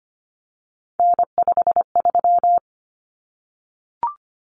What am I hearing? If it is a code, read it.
Morse code, heard as a keyed tone, about 25 wpm, 700 Hz. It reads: N53
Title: Common Cut Numbers - 3-Digit 25wpm